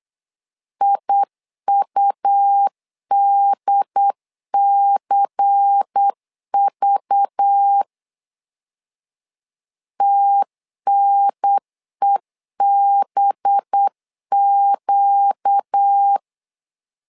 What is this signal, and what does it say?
Morse code, heard as a keyed tone, about 8 wpm, 800 Hz. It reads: IUDCV TNEBQ
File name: LEZIONI / 5 LBVC / 5 LBVC 8wpm 2935